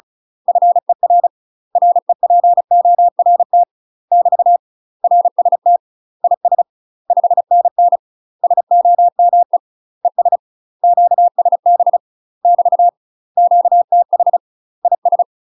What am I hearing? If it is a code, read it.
Morse code, heard as a keyed tone, about 36 wpm, 700 Hz. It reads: FER REPORT <BT> RST IS 5NN SOME ES QSB <BT> QTH IS
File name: UK MASTER QSOs TRAINING / 35WPM / UK MASTER QSOs 0011